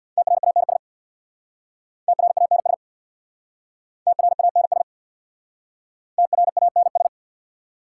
Morse code, heard as a keyed tone, about 62 wpm, 690 Hz. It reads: TRANS TRANS TRANS TRANS
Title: Speed-Racing Prefixes and Suffixes 50wpm